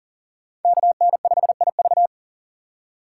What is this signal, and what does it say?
Morse code, heard as a keyed tone, about 38 wpm, 690 Hz. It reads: KN5IV
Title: Code Groups - 5 Letter:Numbers - 40wpm